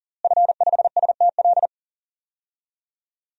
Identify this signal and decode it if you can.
Morse code, heard as a keyed tone, about 40 wpm, 700 Hz. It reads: F5STL
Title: DXCC-100 Call Signs - 5 Characters - Repeated 3x - 40wpm